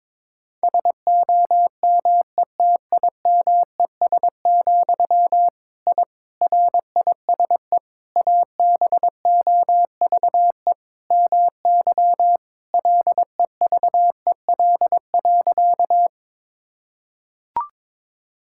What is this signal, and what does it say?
Morse code, heard as a keyed tone, about 22 wpm, 700 Hz. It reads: SOMETIMES, I RISE ABOVE MY LEVEL.
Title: Sentences from Top 1000 Words - Review Part 6 - 22wpm